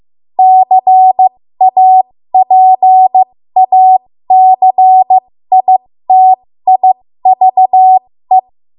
Morse code, heard as a keyed tone, about 15 wpm, spaced as for 13 wpm, 750 Hz. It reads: CAPACITIVE
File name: Morse / 13WPM